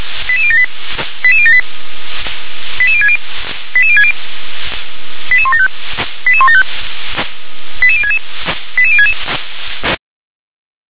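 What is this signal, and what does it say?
Five-tone selective call (ZVEI), heard as quick runs of five tones, tones of about 70 ms. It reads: 88078 88078 88060 88060 88175 88175 88066 88066
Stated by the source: ZVEI five-tone selective call received on SDR.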